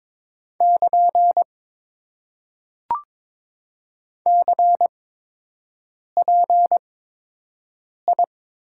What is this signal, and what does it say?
Morse code, heard as a keyed tone, about 22 wpm, spaced as for 6 wpm, 700 Hz. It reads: <KN> CPI
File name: Top 90 QSO Elements - 8x Character Spacing - 22wpm